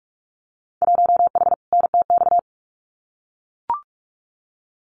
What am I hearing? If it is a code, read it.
Morse code, heard as a keyed tone, about 44 wpm, 700 Hz. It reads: 1H NTX
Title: Winter Field Day Exchanges 45wpm